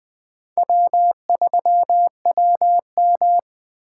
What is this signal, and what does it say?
Morse code, heard as a keyed tone, about 20 wpm, 700 Hz. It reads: W3WM
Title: Speed Racing - USA 1x2 Call Signs 17wpm